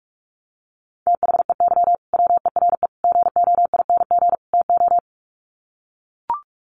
Morse code, heard as a keyed tone, about 44 wpm, 700 Hz. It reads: THEY WERE GOING TO
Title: Sets of 4 Words 45wpm